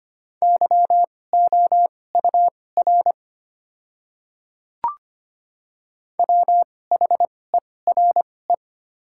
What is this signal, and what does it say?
Morse code, heard as a keyed tone, about 25 wpm, spaced as for 18 wpm, 700 Hz. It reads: YOUR WHERE
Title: 2x Character Spacing - Top 200 Words - 25wpm